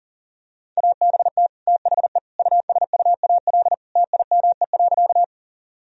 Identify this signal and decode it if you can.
Morse code, heard as a keyed tone, about 40 wpm, 700 Hz. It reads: ABT THE USUAL TIME.
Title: Sentences from Top 400 Words and Abbreviations 40wpm